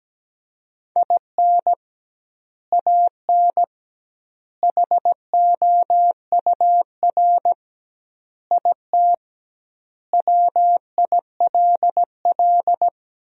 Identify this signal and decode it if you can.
Morse code, heard as a keyed tone, about 17 wpm, 700 Hz. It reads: IN AN HOUR IT WILL
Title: 2x Word-Spacing - Sentences from Top 400 Words - 17wpm